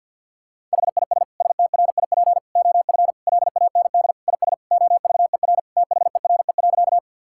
Morse code, heard as a keyed tone, about 50 wpm, 700 Hz. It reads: HIS STRIP OF LAND IS OVER THERE.
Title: Sentences from Top 1600 Words 50wpm